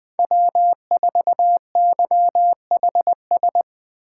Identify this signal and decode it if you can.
Morse code, heard as a keyed tone, about 20 wpm, 700 Hz. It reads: W4YHS
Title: USA Call Signs - 5 Characters - 8x Character Spacing - 20wpm